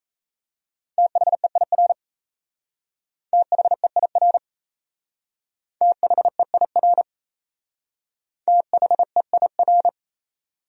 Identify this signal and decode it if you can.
Morse code, heard as a keyed tone, about 34 wpm, 700 Hz. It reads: THEIR THEIR THEIR THEIR
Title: Speed-Racing Top 200 Words 28wpm